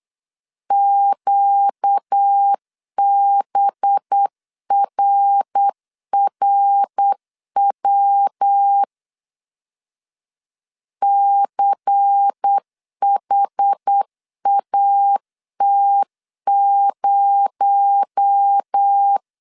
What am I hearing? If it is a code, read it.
Morse code, heard as a keyed tone, about 8 wpm, 800 Hz. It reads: QBRRW CHAT0
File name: CW / DET / 1 6 1500C S B 8wpm 5142